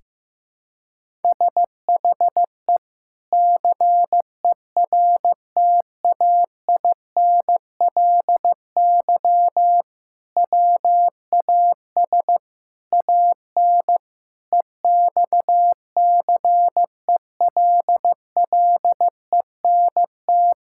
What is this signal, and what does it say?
Morse code, heard as a keyed tone, about 15 wpm, 700 Hz. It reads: SHE CERTAINLY WAS AN EXCELLENT
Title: Sentences from Top 1000 Words - Review Part 6 - 15wpm